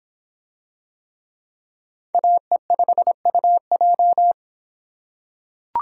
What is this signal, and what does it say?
Morse code, heard as a keyed tone, about 25 wpm, 700 Hz. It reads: AE5UJ